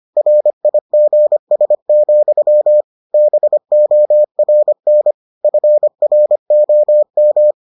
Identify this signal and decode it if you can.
Morse code, heard as a keyed tone, about 25 wpm, 600 Hz. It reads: RIGS, BORN FROM